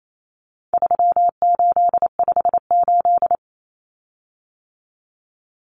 Morse code, heard as a keyed tone, about 28 wpm, 700 Hz. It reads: 3858
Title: Common Cut Numbers - 4-Digit 28wpm